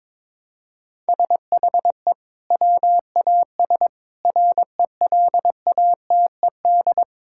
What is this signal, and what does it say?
Morse code, heard as a keyed tone, about 22 wpm, 700 Hz. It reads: SHE WAS RELATED
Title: Sentences from Top 1000 Words - Review Part 5 - 22wpm